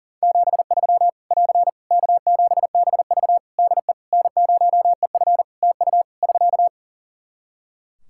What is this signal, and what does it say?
Morse code, heard as a keyed tone, about 40 wpm, 700 Hz. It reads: 73 <AR> K7BV DE N0EF TU <SK>
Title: Practice QSO - Exchange 4 - Ending - 40wpm